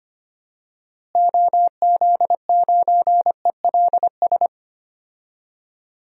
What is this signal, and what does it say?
Morse code, heard as a keyed tone, about 25 wpm, 700 Hz. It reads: OZ9ELS
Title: DXCC-100 Call Signs - 6 Characters - Repeated 2x - 25wpm